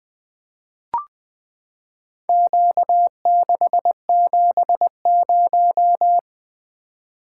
Morse code, heard as a keyed tone, about 20 wpm, 700 Hz. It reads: Q670
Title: Four Letter-Number Combinations - 20wpm